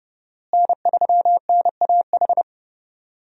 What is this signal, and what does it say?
Morse code, heard as a keyed tone, about 30 wpm, 700 Hz. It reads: N3NAH
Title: USA Call Signs - 5 Characters - Repeated 2x - 30wpm